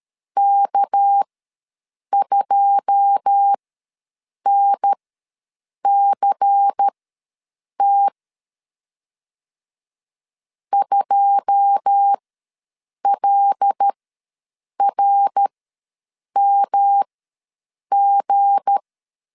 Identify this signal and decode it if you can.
Morse code, heard as a keyed tone, about 13 wpm, spaced as for 7 wpm, 800 Hz. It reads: K2NCT 2LRMG
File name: LEZIONI / 9 2 ws 12wpm 3406